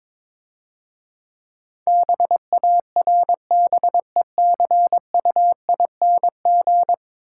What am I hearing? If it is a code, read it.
Morse code, heard as a keyed tone, about 22 wpm, 690 Hz. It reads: BARBECUING
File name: Easy Words - Speed-Racing with Hints - 22wpm